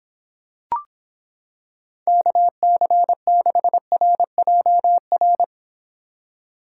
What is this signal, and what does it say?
Morse code, heard as a keyed tone, about 26 wpm, 700 Hz. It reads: KC6RJR